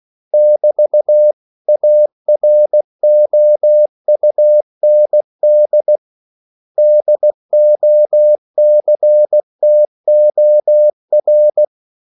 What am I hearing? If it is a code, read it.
Morse code, heard as a keyed tone, about 16 wpm, 600 Hz. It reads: <BT> AROUND DOCTOR
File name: CW / Practice / 500MostCommonEnglishWords16